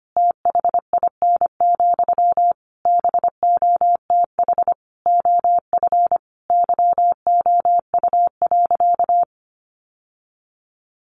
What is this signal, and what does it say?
Morse code, heard as a keyed tone, about 25 wpm, 700 Hz. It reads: THIN, BOTH OF YOU.
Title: Sentences from Top 1200 Words 25wpm